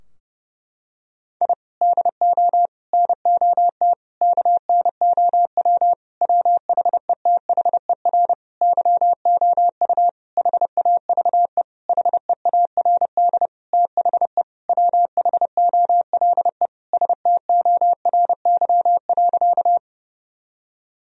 Morse code, heard as a keyed tone, about 30 wpm, 700 Hz. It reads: I DO NOT KNOW WHETHER YOU HAVE HEARD THE WHOLE STORY.
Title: Sentences from Top 1000 Words - Review Part 3 - 30wpm